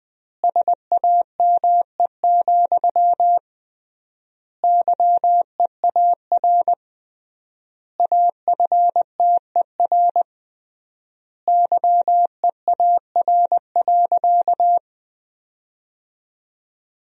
Morse code, heard as a keyed tone, about 21 wpm, 700 Hz. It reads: SAME, YEAR AFTER YEAR.
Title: 3x Word-Spacing - Sentences from Top 300 Words - 20wpm